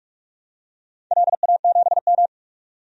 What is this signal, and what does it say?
Morse code, heard as a keyed tone, about 44 wpm, 700 Hz. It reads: LA7M
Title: DXCC-100 Call Signs - 4 Characters - 45wpm